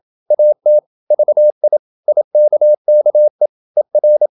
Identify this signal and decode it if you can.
Morse code, heard as a keyed tone, about 27 wpm, 600 Hz. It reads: AT VI IKKE ER
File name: Ferie QRQ 27wpm